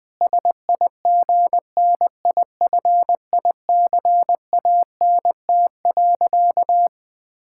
Morse code, heard as a keyed tone, about 20 wpm, 700 Hz. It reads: SIGNIFICANT.
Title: Sentences from Top 1000 Words - Review Part 6 - 20wpm